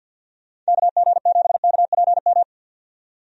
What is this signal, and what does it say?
Morse code, heard as a keyed tone, about 47 wpm, 700 Hz. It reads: KG7KPM
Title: USA Call Signs - 6 Characters - 50wpm